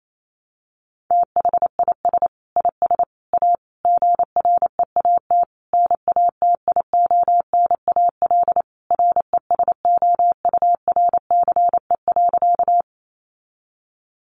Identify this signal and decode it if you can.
Morse code, heard as a keyed tone, about 28 wpm, 700 Hz. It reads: THIS IS A GREAT NATIONAL RESOURCE.
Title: Sentences from Top 1300 Words 28wpm